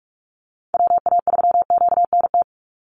Morse code, heard as a keyed tone, about 44 wpm, 700 Hz. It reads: WA3QNT
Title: Call Signs Repeated 2x - 45wpm